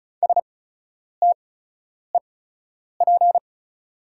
Morse code, heard as a keyed tone, about 36 wpm, spaced as for 10 wpm, 690 Hz. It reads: STEP
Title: 8x Character Spacing - Top 500 Words - 35wpm